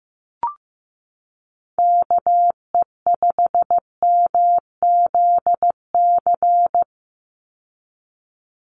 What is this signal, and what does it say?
Morse code, heard as a keyed tone, about 15 wpm, 700 Hz. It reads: KE5MZC
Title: USA Call Signs 15wpm